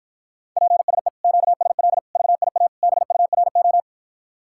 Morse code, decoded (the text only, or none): PSE QSL VIA BURO